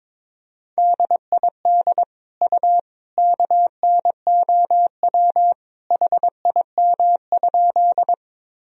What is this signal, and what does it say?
Morse code, heard as a keyed tone, about 22 wpm, 700 Hz. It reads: DID U KNOW HIM?
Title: Sentences from Top 100 Words and Abbreviations 22wpm